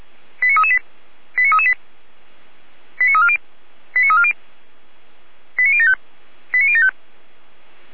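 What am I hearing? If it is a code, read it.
Five-tone selective call (ZVEI), heard as quick runs of five tones, tones of about 70 ms. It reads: 89338 89338 89340 89340 89075 89075
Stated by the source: ZVEI five-tone call 89325 received via SDR.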